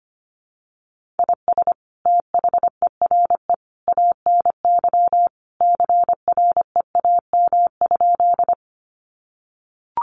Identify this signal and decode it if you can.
Morse code, heard as a keyed tone, about 25 wpm, 700 Hz. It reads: IS THERE ANY CREAM?
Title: Sentences from Top 1500 Words 25wpm